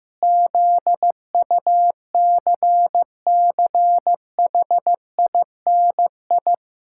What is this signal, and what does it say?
Morse code, heard as a keyed tone, about 15 wpm, 700 Hz. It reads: ZUCCHINI
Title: Pangram Challenge v2 - 15wpm